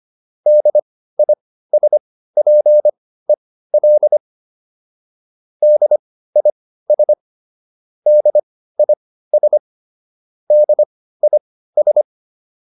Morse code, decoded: DISPEL DIS DIS DIS

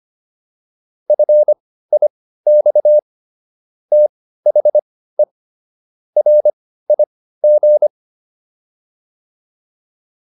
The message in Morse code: FIX THE RIG

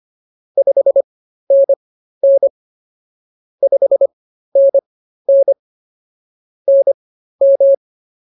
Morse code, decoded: 5NN 5NN NM

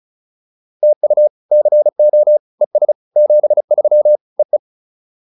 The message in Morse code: TU CO ES 73 EE

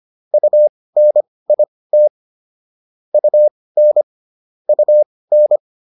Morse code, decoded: UNIT UN UN

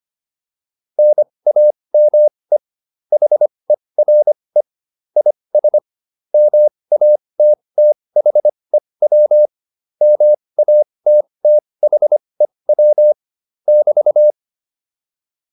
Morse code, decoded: NAME HERE IS MATTHEW MATTHEW <BT>

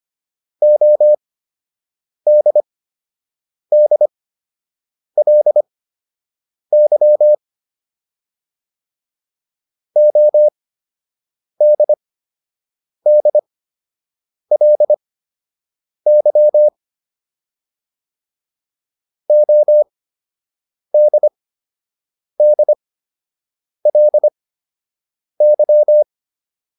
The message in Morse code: ODDLY ODDLY ODDLY

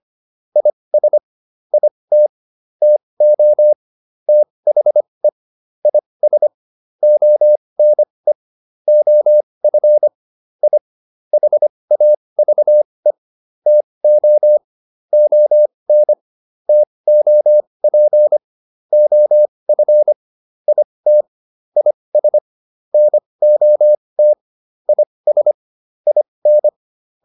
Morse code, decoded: IS IT TO THE IS ONE OF I HAVE TO ON TOP OF IT IS NOT IS IN

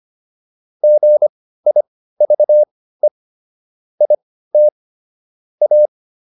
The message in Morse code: GIVE IT A

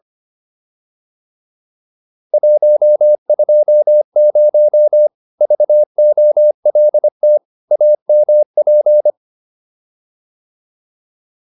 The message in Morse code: 120 VOLT AMP

